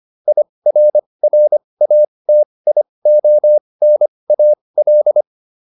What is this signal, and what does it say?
Morse code, heard as a keyed tone, about 25 wpm, spaced as for 20 wpm, 600 Hz. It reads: IRRATIONAL